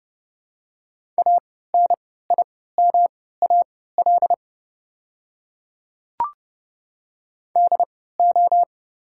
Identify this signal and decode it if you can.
Morse code, heard as a keyed tone, about 30 wpm, spaced as for 17 wpm, 690 Hz. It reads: ANIMAL DO
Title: 3x Character Spacing - Top 200 Words - 30wpm